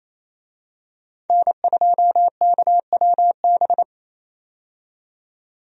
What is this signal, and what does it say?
Morse code, heard as a keyed tone, about 27 wpm, 700 Hz. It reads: N2KWB
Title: USA Call Signs - 5 Characters - 28wpm